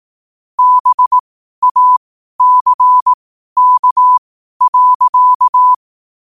Morse code, heard as a keyed tone, about 18 wpm, spaced as for 12 wpm, 1000 Hz. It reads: BACK.
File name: qotdfiles / 12wpm